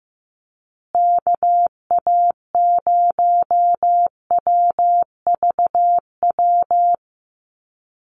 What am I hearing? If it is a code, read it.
Morse code, heard as a keyed tone, about 15 wpm, 700 Hz. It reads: KA0WVW
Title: USA Call Signs Repeated 3x 15wpm